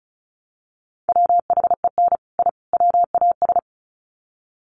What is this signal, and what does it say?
Morse code, heard as a keyed tone, about 34 wpm, 700 Hz. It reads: WHEN I WAS
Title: Sets of 3 Words 35wpm